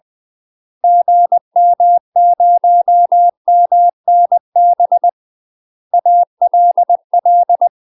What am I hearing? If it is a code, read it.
Morse code, heard as a keyed tone, about 20 wpm, 700 Hz. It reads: GM0MNB ALL
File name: UK MASTER QSOs TRAINING / 20WPM / UK MASTER QSOs 0065